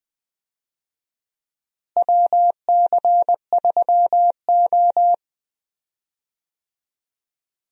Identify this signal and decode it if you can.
Morse code, heard as a keyed tone, about 20 wpm, 700 Hz. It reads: WC3O
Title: USA Call Signs - 4 Characters - Repeated 2x - 20wpm